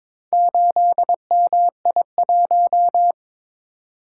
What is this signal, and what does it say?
Morse code, heard as a keyed tone, about 22 wpm, 700 Hz. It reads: 8MI1
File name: practice / 22wpm / Four Letter-Number Combinations - 22wpm